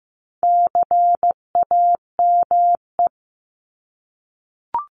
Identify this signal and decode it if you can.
Morse code, heard as a keyed tone, about 15 wpm, 700 Hz. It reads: CAME
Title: Top 200 Words Repeated 15wpm